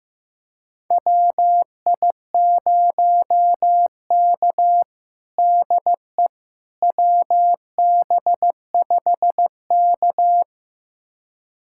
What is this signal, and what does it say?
Morse code, heard as a keyed tone, about 15 wpm, 700 Hz. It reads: WI0K DE WB5K
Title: USA Callsign DE Callsign - 4 Characters - 15wpm